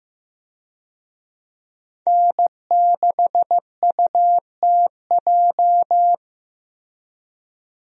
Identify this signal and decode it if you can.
Morse code, heard as a keyed tone, about 15 wpm, 700 Hz. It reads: N6UTJ
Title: USA Call Signs - 5 Characters - Repeated 2x - 15wpm